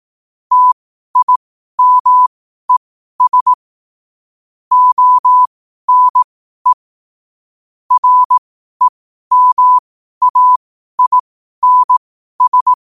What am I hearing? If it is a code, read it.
Morse code, heard as a keyed tone, about 18 wpm, spaced as for 12 wpm, 1000 Hz. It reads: TIMES ONE REMAINS
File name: qotdfiles / 12wpm